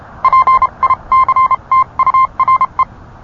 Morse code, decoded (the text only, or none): PICTURE